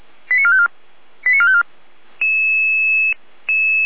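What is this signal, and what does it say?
Five-tone selective call (ZVEI), heard as quick runs of five tones, tones of about 80 ms. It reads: 89454 89454
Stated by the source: ZVEI five-tone selective call received on SDR.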